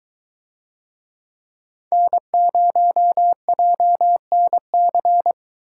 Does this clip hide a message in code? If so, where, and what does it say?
Yes: Morse code from 1.9 to 5.3 s, heard as a keyed tone, about 22 wpm, 700 Hz. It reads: N0JNC